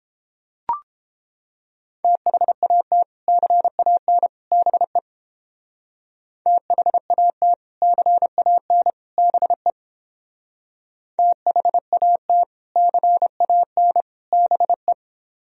Morse code, heard as a keyed tone, about 30 wpm, 700 Hz. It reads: THAT CAN BE THAT CAN BE THAT CAN BE